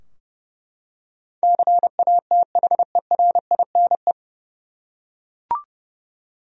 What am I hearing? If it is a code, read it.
Morse code, heard as a keyed tone, about 30 wpm, 700 Hz. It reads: CATHERINE
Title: USA Common Names - 30wpm